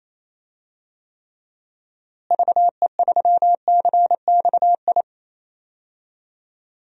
Morse code, heard as a keyed tone, about 28 wpm, 700 Hz. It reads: VE3CXI
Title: DXCC-100 Call Signs - 6 Characters - Repeated 3x - 28wpm